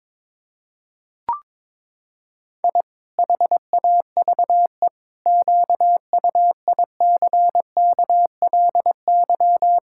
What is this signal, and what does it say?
Morse code, heard as a keyed tone, about 22 wpm, 700 Hz. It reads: I HAVE QUICKLY
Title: Pangram Challenge v2 - 22wpm